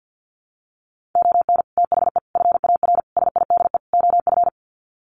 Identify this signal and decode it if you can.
Morse code, heard as a keyed tone, about 50 wpm, 700 Hz. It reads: ON THE FAR SIDE OF
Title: Sets of 5 Words 50wpm